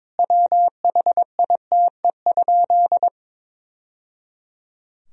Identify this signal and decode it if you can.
Morse code, heard as a keyed tone, about 22 wpm, 700 Hz. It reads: WHITE?
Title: Sentences from Top 1000 Words - Review Part 2 - 22wpm